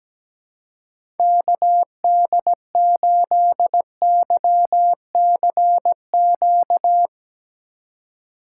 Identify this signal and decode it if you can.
Morse code, heard as a keyed tone, about 17 wpm, 700 Hz. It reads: KD8YCQ
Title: USA Call Signs - 3x Character Spacing - 17wpm